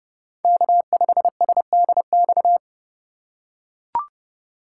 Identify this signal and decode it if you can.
Morse code, heard as a keyed tone, about 30 wpm, 700 Hz. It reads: K5SDX